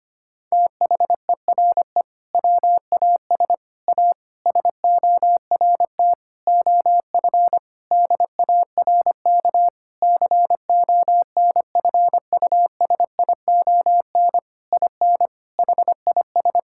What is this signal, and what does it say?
Morse code, heard as a keyed tone, about 25 wpm, 700 Hz. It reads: THERE WAS A SORT OF DARK CONFUSION IN HIS
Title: Sentences from Top 1600 Words 25wpm